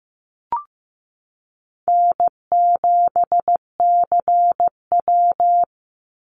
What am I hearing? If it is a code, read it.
Morse code, heard as a keyed tone, about 15 wpm, 700 Hz. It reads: N7CW
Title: NA SOTA Chaser Callsigns 15wpm